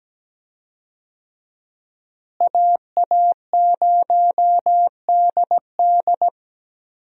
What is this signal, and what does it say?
Morse code, heard as a keyed tone, about 17 wpm, 700 Hz. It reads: AA0DD
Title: USA Call Signs - 5 Characters - Repeated 3x - 17wpm